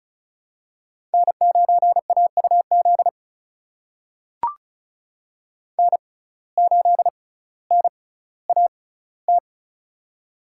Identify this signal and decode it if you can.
Morse code, heard as a keyed tone, about 36 wpm, spaced as for 18 wpm, 700 Hz. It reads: N9AUZ N8NAT
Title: USA Call Signs - 5 Characters - 6x Character Spacing - 35wpm